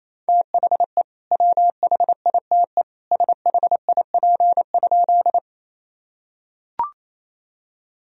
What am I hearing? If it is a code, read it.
Morse code, heard as a keyed tone, about 27 wpm, 700 Hz. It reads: THE WHITE SHIP?
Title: Sentences from Top 1000 Words - Review Part 2 - 28wpm